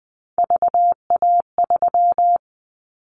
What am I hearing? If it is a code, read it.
Morse code, heard as a keyed tone, about 20 wpm, 700 Hz. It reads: VA3
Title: Callsign Prefixes for Top 50 DXCC - 20wpm